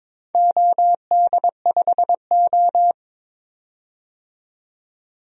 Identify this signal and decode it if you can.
Morse code, heard as a keyed tone, about 22 wpm, 700 Hz. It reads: OD5O
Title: DXCC-100 Call Signs - 4 Characters - 22wpm